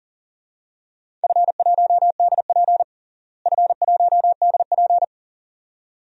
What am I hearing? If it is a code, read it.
Morse code, heard as a keyed tone, about 40 wpm, 700 Hz. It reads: F1DP F1DP
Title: DXCC-100 Call Signs - 4 Characters - Repeated 2x - 40wpm